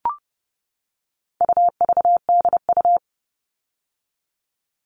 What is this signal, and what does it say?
Morse code, heard as a keyed tone, about 30 wpm, 700 Hz. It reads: UVDU
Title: All Cut Numbers - 4-Digit 30wpm